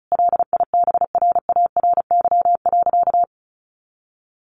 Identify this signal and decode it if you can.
Morse code, heard as a keyed tone, about 34 wpm, 700 Hz. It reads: LIBRARY.
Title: Sentences from Top 600 Words 35wpm